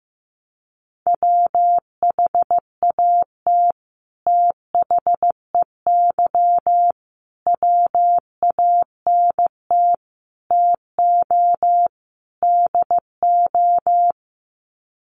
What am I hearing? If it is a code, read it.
Morse code, heard as a keyed tone, about 15 wpm, 700 Hz. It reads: WHAT THEY WANT TO DO
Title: Sets of 5 Words Encore2 15wpm